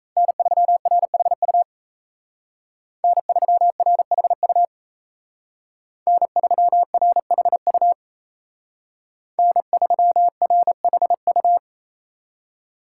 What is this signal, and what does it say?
Morse code, heard as a keyed tone, about 36 wpm, 700 Hz. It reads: N3RHU N3RHU N3RHU N3RHU